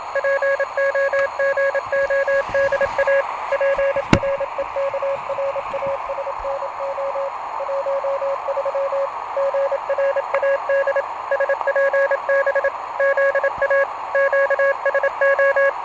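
Morse code (Russian alphabet), heard as a keyed tone, about 27 wpm, 550 Hz, with radio noise. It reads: ПОГОДА ПРЕКРАСНО 13 ГРАД СПБ ЗА ЩСО
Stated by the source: Portable QRP contact on a home-made 3 watt transmitter with a long dipole, received by a station running home-made 1.2 watts, with key clicks and hard keying in question.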